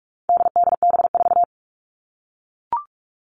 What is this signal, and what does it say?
Morse code, heard as a keyed tone, about 44 wpm, 700 Hz. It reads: DDB4
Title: All Cut Numbers - 4-Digit 45wpm